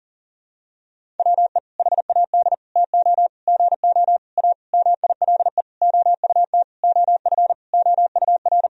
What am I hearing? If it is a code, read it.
Morse code, heard as a keyed tone, about 40 wpm, 700 Hz. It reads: WE HAD TO GO A MILE OUT OF OUR
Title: Sentences from Top 1000 Words - Review Part 1 - 40wpm